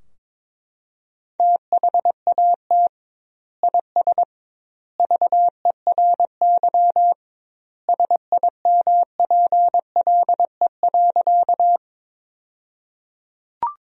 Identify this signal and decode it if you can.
Morse code, heard as a keyed tone, about 22 wpm, 700 Hz. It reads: THAT IS VERY SIMPLE.
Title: 2x Word-Spacing - Sentences from Top 400 Words - 22wpm